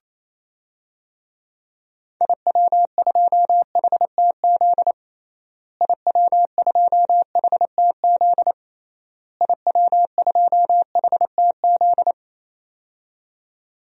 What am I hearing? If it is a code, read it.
Morse code, heard as a keyed tone, about 28 wpm, 700 Hz. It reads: IW2HTZ IW2HTZ IW2HTZ
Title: DXCC-100 Call Signs - 6 Characters - Repeated 3x - 28wpm